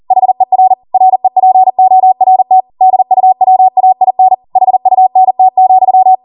Morse code, heard as a keyed tone, about 40 wpm, 750 Hz. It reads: HER REPORT DUWAIN HUNT,